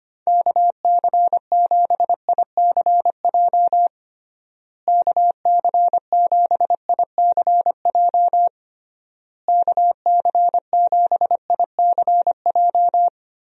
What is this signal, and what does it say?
Morse code, heard as a keyed tone, about 25 wpm, 700 Hz. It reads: KC7ICJ KC7ICJ KC7ICJ
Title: USA Call Signs - 6 Characters - Repeated 3x - 25wpm